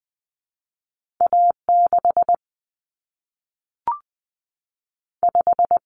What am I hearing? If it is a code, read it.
Morse code, heard as a keyed tone, about 20 wpm, 700 Hz. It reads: A6 5
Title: Common Cut Numbers - 2-Digit 20wpm